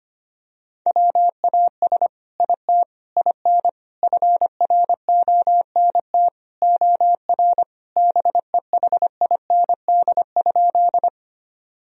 Morse code, heard as a keyed tone, about 25 wpm, 700 Hz. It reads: WAS IT IN FRONT OR BEHIND?
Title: Sentences from Top 1000 Words - Review Part 2 - 25wpm